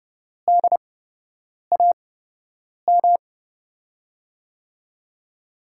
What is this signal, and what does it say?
Morse code, heard as a keyed tone, about 30 wpm, spaced as for 8 wpm, 700 Hz. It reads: DAM
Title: USA Call Signs - 6 Characters - 8x Character Spacing - 30wpm